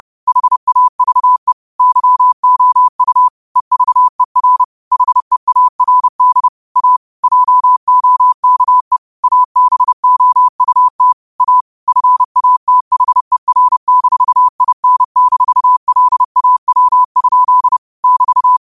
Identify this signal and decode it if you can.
Morse code, heard as a keyed tone, about 30 wpm, 1000 Hz. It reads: HAVE YOU EVER HEARD A JOKE ABOUT A FATHER-IN-LAW? <BT>